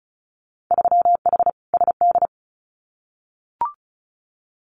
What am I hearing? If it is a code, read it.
Morse code, heard as a keyed tone, about 36 wpm, 700 Hz. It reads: 3H SD
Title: Winter Field Day Exchanges Encore 35wpm